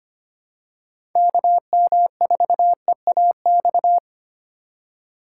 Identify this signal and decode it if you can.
Morse code, heard as a keyed tone, about 25 wpm, 700 Hz. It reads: KM4EAX
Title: USA Call Signs - 6 Characters - 3x Character Spacing - 25wpm